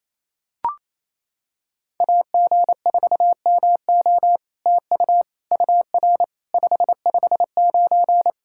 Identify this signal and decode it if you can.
Morse code, heard as a keyed tone, about 28 wpm, 700 Hz. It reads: AG4MO TU UR 559
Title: SOTA QSO Practice 28wpm